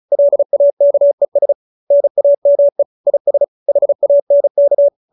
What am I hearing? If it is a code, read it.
Morse code, heard as a keyed tone, about 34 wpm, 570 Hz. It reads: LAKES NAME IS HANK